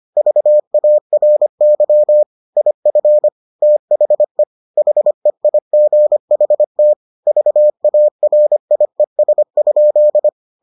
Morse code, heard as a keyed tone, about 25 wpm, 600 Hz. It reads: VARY IF THE HEIGHT VARIES?